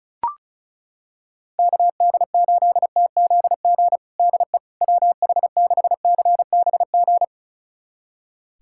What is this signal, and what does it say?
Morse code, heard as a keyed tone, about 36 wpm, 700 Hz. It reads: KD8TZG DE WH6CBG
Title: USA Callsign DE Callsign - 6 Characters - 35wpm